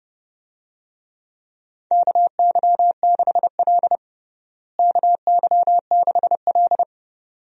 Morse code, heard as a keyed tone, about 30 wpm, 700 Hz. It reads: KY6L KY6L
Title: USA Call Signs - 4 Characters - Repeated 2x - 30wpm